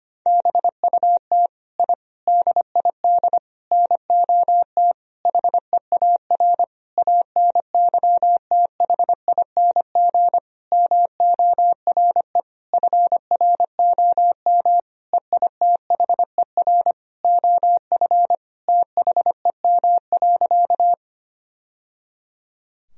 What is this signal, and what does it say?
Morse code, heard as a keyed tone, about 25 wpm, 700 Hz. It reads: BUT I DID NOT HEAR ANYTHING MORE FROM EITHER OF THEM.
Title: Sentences from Top 1000 Words - Review Part 4 - 25wpm